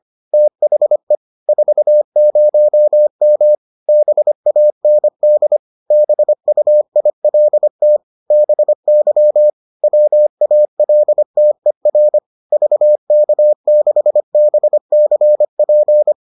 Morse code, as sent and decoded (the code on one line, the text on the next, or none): - .... / ....- ----- -- / -... .- -. -.. / -... ..- .. .-.. - / -... -.-- / .-- .- .-.. - . .-. / ...- -.- -.... -... -.-. .--.
THE 40M BAND BUILT BY WALTER VK6BCP